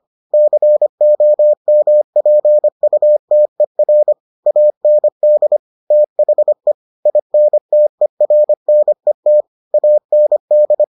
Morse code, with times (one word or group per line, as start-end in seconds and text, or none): 0.3-4.1 COMPUTER
4.5-5.6 AND
5.9-6.7 THE
7.1-9.4 INTERNET
9.7-10.8 AND